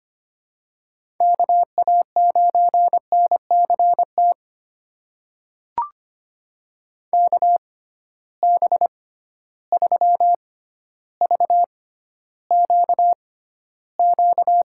Morse code, heard as a keyed tone, about 25 wpm, spaced as for 13 wpm, 700 Hz. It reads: KA9NCT KB3VQQ
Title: USA Call Signs - 6 Characters - 6x Character Spacing - 25wpm